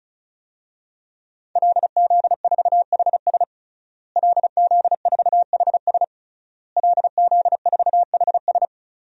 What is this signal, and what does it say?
Morse code, heard as a keyed tone, about 36 wpm, 700 Hz. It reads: LZ4HS LZ4HS LZ4HS
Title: DXCC-100 Call Signs - 5 Characters - Repeated 3x - 35wpm